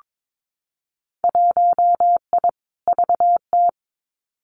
Morse code, heard as a keyed tone, about 22 wpm, 700 Hz. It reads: 1I VT
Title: Winter Field Day Exchanges Encore 22wpm